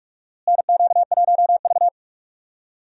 Morse code, heard as a keyed tone, about 41 wpm, 700 Hz. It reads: NQ1V
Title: USA Call Signs - 4 Characters - Repeated 3x - 45wpm